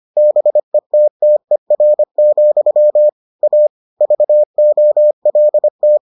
Morse code, heard as a keyed tone, about 25 wpm, 600 Hz. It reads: BETTER, A VOLT